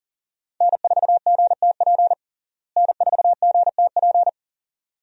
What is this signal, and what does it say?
Morse code, heard as a keyed tone, about 40 wpm, 700 Hz. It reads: N4GTP N4GTP
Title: USA Call Signs - 5 Characters - Repeated 2x - 40wpm